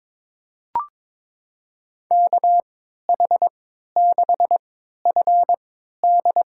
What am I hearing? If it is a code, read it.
Morse code, heard as a keyed tone, about 22 wpm, spaced as for 12 wpm, 700 Hz. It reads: KH6FD
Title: USA Call Signs - 5 Characters - 3x Character Spacing - 22wpm